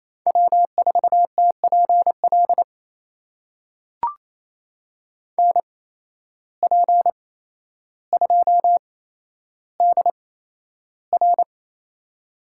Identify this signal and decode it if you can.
Morse code, heard as a keyed tone, about 28 wpm, spaced as for 12 wpm, 700 Hz. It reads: W4TPL NP2DR
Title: USA Call Signs - 5 Characters - 8x Character Spacing - 28wpm